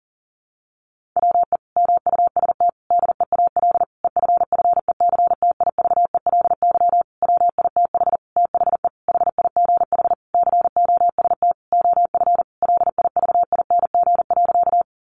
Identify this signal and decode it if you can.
Morse code, heard as a keyed tone, about 37 wpm, 700 Hz. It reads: WE MUST DEAL EFFECTIVELY WITH THE HIGH COST OF LIVING.
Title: Sentences from Top 1400 Words 40wpm